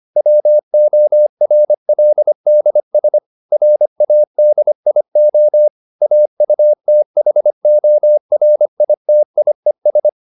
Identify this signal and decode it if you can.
Morse code, heard as a keyed tone, about 25 wpm, 600 Hz. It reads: WORLDS RADIO AUTHORITIES